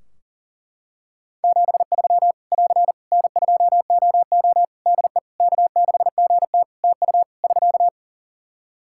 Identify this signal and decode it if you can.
Morse code, heard as a keyed tone, about 40 wpm, 700 Hz. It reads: 73 <AR> N2OO DE K6GT TU <SK>
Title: Practice QSO - Exchange 4 - Ending - 40wpm